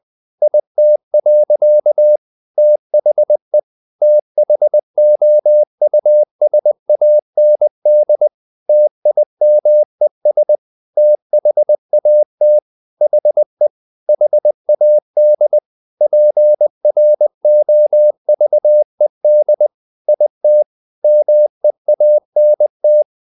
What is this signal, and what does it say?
Morse code, heard as a keyed tone, about 20 wpm, 600 Hz. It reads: IT. THE THOUSAND TIMES THAT HE HAD PROVED IT MEANT
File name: the old man and the sea / the old man and the sea part 1 20wpm